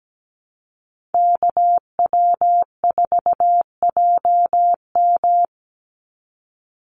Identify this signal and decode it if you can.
Morse code, heard as a keyed tone, about 17 wpm, 700 Hz. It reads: KW4JM
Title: NA SOTA Activator Callsigns 17wpm